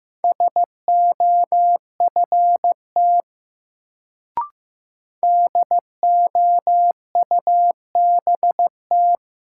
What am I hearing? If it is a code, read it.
Morse code, heard as a keyed tone, about 15 wpm, 700 Hz. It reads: SOFT DOUBT
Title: Sentences from Top 1500 Words - New Word Drill Alt - 15wpm